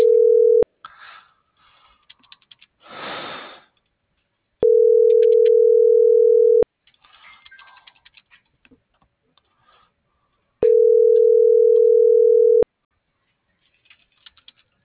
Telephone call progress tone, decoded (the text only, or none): ringback tone